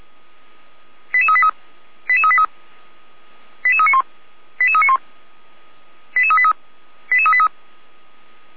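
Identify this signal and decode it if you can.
Five-tone selective call (ZVEI), heard as quick runs of five tones, tones of about 70 ms. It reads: 88383 88383 88391 88391 88383 88383
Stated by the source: ZVEI five-tone call 88355 received on SDR.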